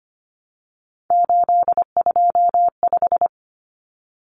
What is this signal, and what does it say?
Morse code, heard as a keyed tone, about 25 wpm, 700 Hz. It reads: 825
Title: Common Cut Numbers - 3-Digit 25wpm